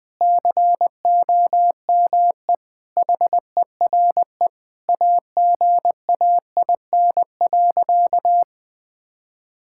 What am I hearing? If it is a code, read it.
Morse code, heard as a keyed tone, about 21 wpm, 700 Hz. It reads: COME HERE AGAIN.
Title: Sentences from Top 1600 Words 20wpm